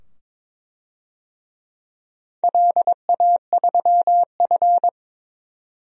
Morse code, heard as a keyed tone, about 23 wpm, 690 Hz. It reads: LA3F
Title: DXCC-100 Call Signs - 4 Characters - 22wpm